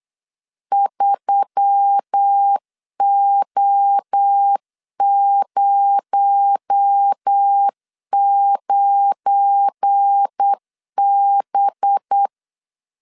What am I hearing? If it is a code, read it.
Morse code, heard as a keyed tone, about 8 wpm, 800 Hz. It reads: 3O09B
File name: LEZIONI / 7 GF9Z / 7 GF9 8wpm 3545